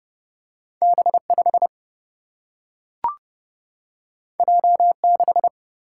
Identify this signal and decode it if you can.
Morse code, heard as a keyed tone, about 30 wpm, 700 Hz. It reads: B5 J6
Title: Single-Letter Single-Number 30wpm